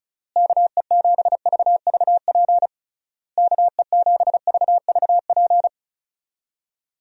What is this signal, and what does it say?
Morse code, heard as a keyed tone, about 34 wpm, 700 Hz. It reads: KE7VVP KE7VVP
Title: USA Call Signs - 6 Characters - Repeated 3x - 35wpm